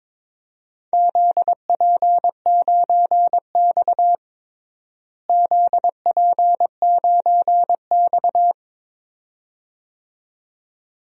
Morse code, heard as a keyed tone, about 22 wpm, 700 Hz. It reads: ZP9X ZP9X
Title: DXCC-100 Call Signs - 4 Characters - Repeated 2x - 22wpm